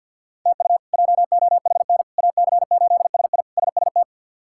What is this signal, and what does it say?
Morse code, heard as a keyed tone, about 48 wpm, 700 Hz. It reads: TU JOHN AC8SI SST
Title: K1USN SST QSO Practice - 50wpm